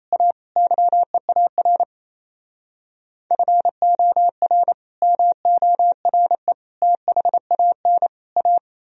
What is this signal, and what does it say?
Morse code, heard as a keyed tone, about 37 wpm, 700 Hz. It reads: A YEAR FOR MORE THAN A